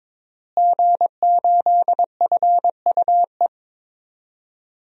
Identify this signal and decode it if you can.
Morse code, heard as a keyed tone, about 22 wpm, 700 Hz. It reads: G8FUE
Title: DXCC-100 Call Signs - 5 Characters - Repeated 3x - 22wpm